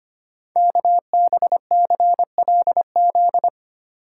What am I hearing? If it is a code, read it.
Morse code, heard as a keyed tone, about 25 wpm, 700 Hz. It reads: KBCLZ
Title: Code Groups - 5 Letters - 25wpm